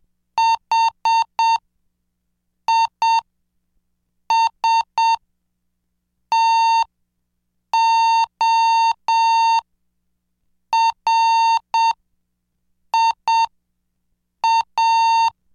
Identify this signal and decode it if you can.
Morse code, heard as a keyed tone, about 7 wpm, spaced as for 5 wpm, 910 Hz. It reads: HISTORIA
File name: cursos / CDestudo / 26 Track 26